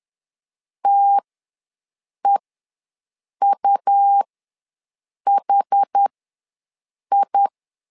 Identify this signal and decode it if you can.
Morse code, heard as a keyed tone, about 10 wpm, spaced as for 6 wpm, 800 Hz. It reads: TEUHI
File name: LEZIONI / 2 UHDN / 2 UH ws 10wpm 2029